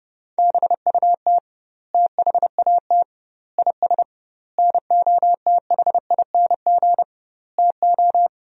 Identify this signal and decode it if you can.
Morse code, heard as a keyed tone, about 30 wpm, 700 Hz. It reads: BUT THAT IS NOTHING TO
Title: 2x Word-Spacing - Sentences from Top 500 Words - 30wpm